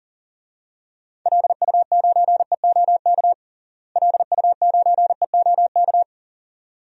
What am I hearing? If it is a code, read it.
Morse code, heard as a keyed tone, about 40 wpm, 700 Hz. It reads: LU9EOK LU9EOK
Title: DXCC-100 Call Signs - 6 Characters - Repeated 2x - 40wpm